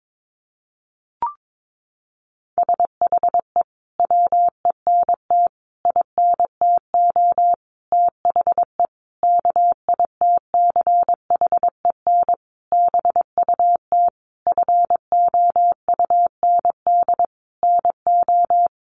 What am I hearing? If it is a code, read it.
Morse code, heard as a keyed tone, about 22 wpm, 700 Hz. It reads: SHE WENT INTO THE KITCHEN BUT FOUND NO
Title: Sentences from Top 1400 Words 22wpm